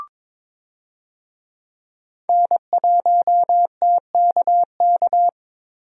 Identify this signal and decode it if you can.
Morse code, heard as a keyed tone, about 22 wpm, 700 Hz. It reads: N1TKK
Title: USA Call Signs - 5 Characters - Repeated 3x - 22wpm